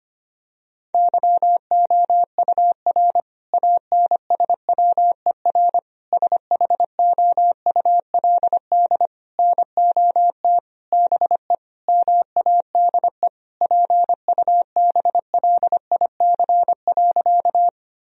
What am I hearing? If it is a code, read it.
Morse code, heard as a keyed tone, about 25 wpm, 700 Hz. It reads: YOUR ANSWER SHOULD NOT BE MADE PUBLIC.
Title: Sentences from Top 1000 Words - Review Part 3 - 25wpm